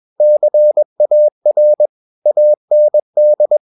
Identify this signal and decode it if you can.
Morse code, heard as a keyed tone, about 21 wpm, 600 Hz. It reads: CAR AND
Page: CW med Gnister